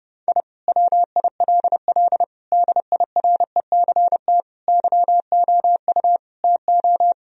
Morse code, heard as a keyed tone, about 30 wpm, 700 Hz. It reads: I WILL DIRECT YOU TO
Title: Sentences from Top 1000 Words - Review Part 2 - 30wpm